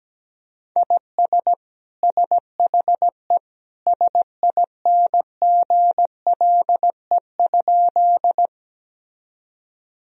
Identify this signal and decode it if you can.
Morse code, heard as a keyed tone, about 17 wpm, 700 Hz. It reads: IS SHE SINGLE?
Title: Sentences from Top 1000 Words - Review Part 4 - 17wpm